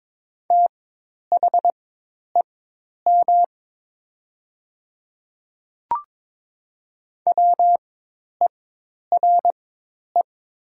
Morse code, heard as a keyed tone, about 22 wpm, spaced as for 10 wpm, 700 Hz. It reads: THEM WERE
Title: 4x Character Spacing - Top 100 Words - 22wpm